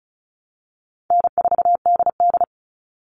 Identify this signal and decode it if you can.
Morse code, heard as a keyed tone, about 34 wpm, 700 Hz. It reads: N4DD
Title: Call Signs Repeated 2x - 35wpm